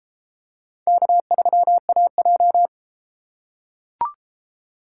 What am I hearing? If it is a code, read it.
Morse code, heard as a keyed tone, about 30 wpm, 700 Hz. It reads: K3AJ